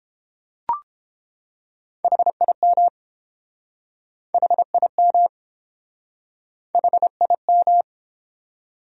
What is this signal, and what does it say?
Morse code, heard as a keyed tone, about 32 wpm, 700 Hz. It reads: HIM HIM HIM